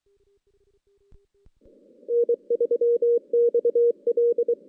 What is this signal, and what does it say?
Morse code, heard as a keyed tone, about 23 wpm, 470 Hz. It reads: N3XL
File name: 2016 JAN CW NAQP / 2016 Jan naqp k5ot - 0-6